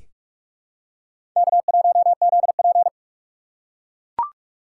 Morse code, heard as a keyed tone, about 43 wpm, 700 Hz. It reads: K1ZP
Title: USA Call Signs - 4 Characters - 3x Character Spacing - 45wpm